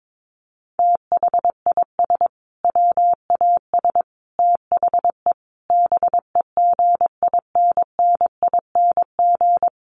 Morse code, heard as a keyed tone, about 22 wpm, 700 Hz. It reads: THIS WAS THE BEGINNING
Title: Sentences from Top 900 Words 22wpm